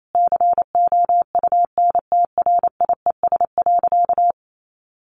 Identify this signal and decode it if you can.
Morse code, heard as a keyed tone, about 28 wpm, 700 Hz. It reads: COUNTRIES.
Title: A Sound Of Thunder 28wpm